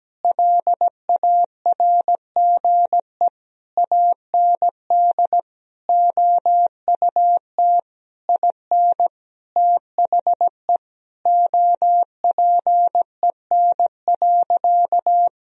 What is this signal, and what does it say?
Morse code, heard as a keyed tone, about 17 wpm, 700 Hz. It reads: LARGE AND OUT IN THE OPEN.
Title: Sentences from Top 1000 Words - Review Part 1 - 17wpm